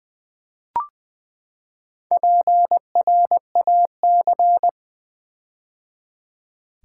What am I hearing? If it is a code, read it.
Morse code, heard as a keyed tone, about 20 wpm, 700 Hz. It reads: PRAC
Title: Build A Word - 4 Letters At A Time - 12-Letter Words - 20wpm